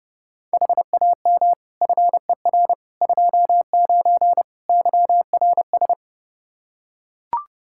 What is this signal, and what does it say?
Morse code, heard as a keyed tone, about 30 wpm, 700 Hz. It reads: HAM FER 29 YRS
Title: K1USN SST QSO Practice 30wpm